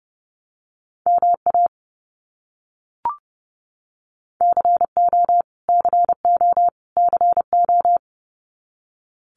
Morse code, heard as a keyed tone, about 30 wpm, 700 Hz. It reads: MA CO CO CO
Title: US States Repeated 3x 30wpm